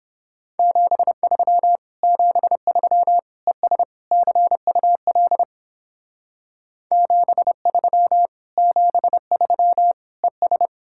Morse code, heard as a keyed tone, about 31 wpm, 700 Hz. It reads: 73 73 ES CUL 73 73 ES